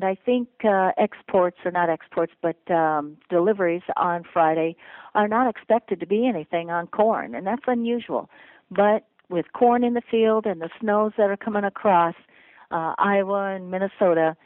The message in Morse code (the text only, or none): none